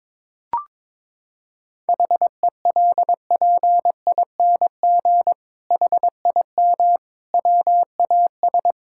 Morse code, heard as a keyed tone, about 22 wpm, 700 Hz. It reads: HELPING HIM WAS
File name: Sentences with 7-Word Phrases - 22wpm